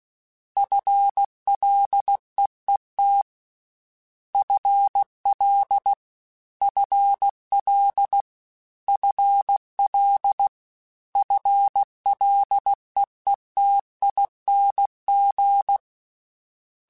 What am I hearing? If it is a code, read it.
Morse code, heard as a keyed tone, about 16 wpm, 800 Hz. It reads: FLEET FL FL FL FLEETING
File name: FL Prefixes 3x2 @15